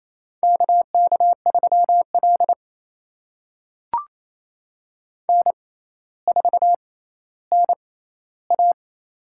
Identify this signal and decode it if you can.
Morse code, heard as a keyed tone, about 28 wpm, spaced as for 14 wpm, 700 Hz. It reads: KK3L N4NA
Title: USA Call Signs - 4 Characters - 6x Character Spacing - 28wpm